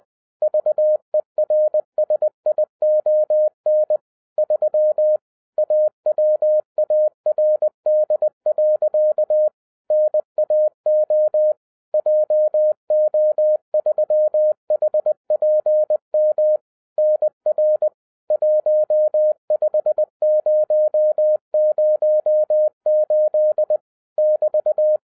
Morse code, heard as a keyed tone, about 20 wpm, 610 Hz. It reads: VERSION 3 AWARD. NAO JO3HPM NR 15008 <BT>